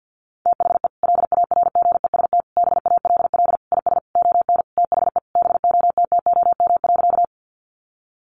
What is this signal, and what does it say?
Morse code, heard as a keyed tone, about 50 wpm, 700 Hz. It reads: THE LARGEST BALL IS ON THE BOTTOM.
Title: Easy Phrases 50wpm